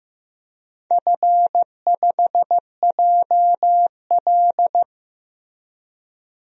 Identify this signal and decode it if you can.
Morse code, heard as a keyed tone, about 15 wpm, 700 Hz. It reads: F5JL
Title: DXCC-100 Call Signs - 4 Characters - 15wpm